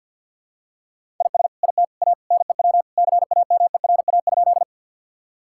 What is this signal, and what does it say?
Morse code, heard as a keyed tone, about 52 wpm, 700 Hz. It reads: IS IT A NEW CAMERA?
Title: Sentences from Top 1000 Words - Review Part 5 - 50wpm